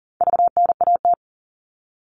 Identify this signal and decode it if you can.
Morse code, heard as a keyed tone, about 40 wpm, 700 Hz. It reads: VNAT
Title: All Cut Numbers - 4-Digit 40wpm